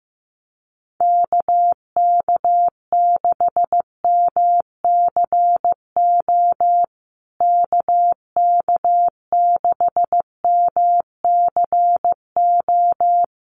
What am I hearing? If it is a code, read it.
Morse code, heard as a keyed tone, about 15 wpm, 700 Hz. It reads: KK6MCO KK6MCO
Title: USA Call Signs Repeated 3x 15wpm